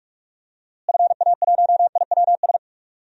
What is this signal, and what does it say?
Morse code, heard as a keyed tone, about 46 wpm, 700 Hz. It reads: FA1IWS
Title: DXCC-100 Call Signs - 6 Characters - Repeated 3x - 45wpm